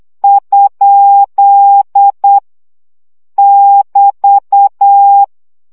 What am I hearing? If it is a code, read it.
Morse code, heard as a keyed tone, about 8 wpm, 800 Hz. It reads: ? <BT>
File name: CW gyakorlas kerdojel 7 baud